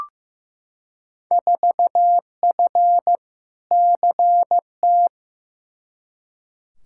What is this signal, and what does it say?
Morse code, heard as a keyed tone, about 15 wpm, 700 Hz. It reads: 4F CT
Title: Field Day Exchanges Encore - 15wpm